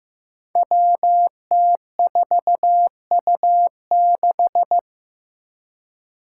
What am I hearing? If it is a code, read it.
Morse code, heard as a keyed tone, about 15 wpm, 700 Hz. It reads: WT4U6
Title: Five Challenging Letter-Number Combinations 15wpm